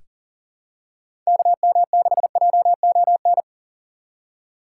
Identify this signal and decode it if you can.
Morse code, heard as a keyed tone, about 40 wpm, 700 Hz. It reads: KM6JON
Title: USA Call Signs - 6 Characters - 4x Character Spacing - 40wpm